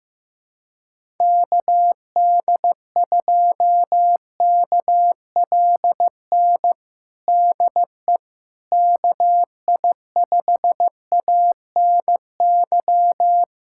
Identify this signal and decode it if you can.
Morse code, heard as a keyed tone, about 15 wpm, 700 Hz. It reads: KD2KLN DE KI5ANY
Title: USA Callsign DE Callsign - 6 Characters - 15wpm